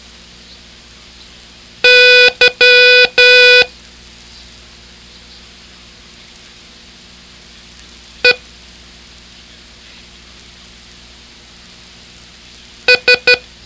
Morse code, heard as a keyed tone, about 12 wpm, 2500 Hz. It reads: Y E S
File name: ham radio / CodeQuick / Code Quick 02